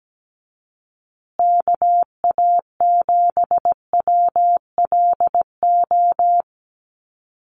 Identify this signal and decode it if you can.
Morse code, heard as a keyed tone, about 17 wpm, 700 Hz. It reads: KA7WLO
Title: USA Call Signs Repeated 2x 17wpm